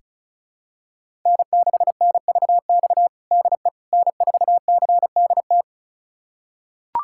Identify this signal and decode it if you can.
Morse code, heard as a keyed tone, about 34 wpm, 690 Hz. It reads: N6NVX DE N4CDT
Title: USA Callsign DE Callsign - 5 Characters - 35wpm